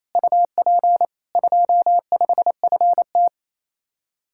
Morse code, heard as a keyed tone, about 35 wpm, 700 Hz. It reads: UP 25FT
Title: Speed-Racing Common QSO Phrases Encore - 28wpm